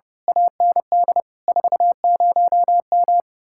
Morse code, heard as a keyed tone, about 33 wpm, 700 Hz. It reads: AND 40M
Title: Speed-Racing Common QSO Phrases - 30wpm